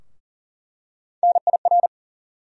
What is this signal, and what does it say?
Morse code, heard as a keyed tone, about 40 wpm, 700 Hz. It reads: NIR
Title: USA Call Sign Postfixes - 3 Letters 40wpm